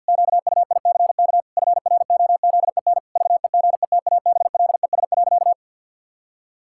Morse code, heard as a keyed tone, about 50 wpm, 700 Hz. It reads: QUICK FROZEN VEGETABLES.
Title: Pangram Challenge v2 - 50wpm